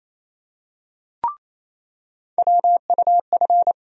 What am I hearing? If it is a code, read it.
Morse code, heard as a keyed tone, about 28 wpm, 700 Hz. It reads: WUF
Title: USA Call Sign Postfixes - 3 Letters 28wpm